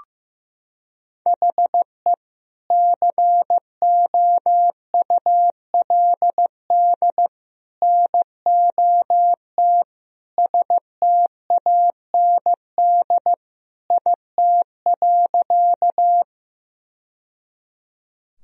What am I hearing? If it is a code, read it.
Morse code, heard as a keyed tone, about 15 wpm, 700 Hz. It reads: HE COULD NOT STAND IT.
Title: Sentences from Top 1000 Words - Review Part 1 - 15wpm